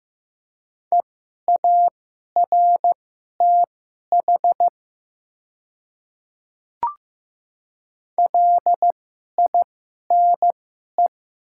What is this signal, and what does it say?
Morse code, heard as a keyed tone, about 15 wpm, spaced as for 11 wpm, 700 Hz. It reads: EARTH LINE
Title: 2x Character Spacing - Top 500 Words - 15wpm